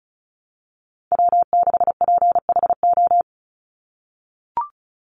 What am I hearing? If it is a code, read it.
Morse code, heard as a keyed tone, about 34 wpm, 700 Hz. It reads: W6PHO
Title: Call Signs - 35wpm